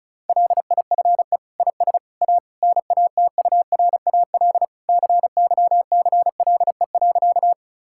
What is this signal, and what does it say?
Morse code, heard as a keyed tone, about 36 wpm, 700 Hz. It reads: LIFE IS A NATURAL CYCLE.
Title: Sentences from Top 1000 Words - Review Part 6 - 35wpm